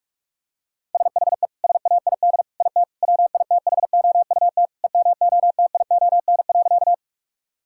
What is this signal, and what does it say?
Morse code, heard as a keyed tone, about 46 wpm, 700 Hz. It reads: SHE SAID IT WITHOUT EMOTION.
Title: Sentences from Top 1600 Words 45wpm